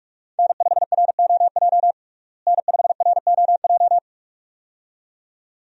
Morse code, heard as a keyed tone, about 47 wpm, 700 Hz. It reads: N5ROJ N5ROJ
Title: USA Call Signs - 5 Characters - Repeated 2x - 45wpm